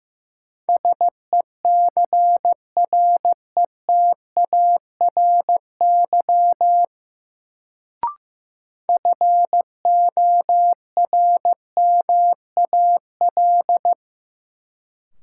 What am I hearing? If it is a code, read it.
Morse code, heard as a keyed tone, about 16 wpm, 700 Hz. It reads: SECRETARY FORMAL
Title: Sentences from Top 1300 Words - New Word Drill - 15wpm